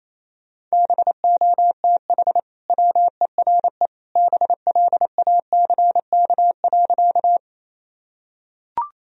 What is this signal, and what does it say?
Morse code, heard as a keyed tone, about 28 wpm, 700 Hz. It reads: BOTH WERE BLACK.
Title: Sentences from Top 1000 Words - Review Part 2 - 28wpm